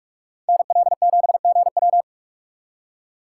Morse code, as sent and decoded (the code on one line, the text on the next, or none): -. .-.. --... --. .--
NL7GW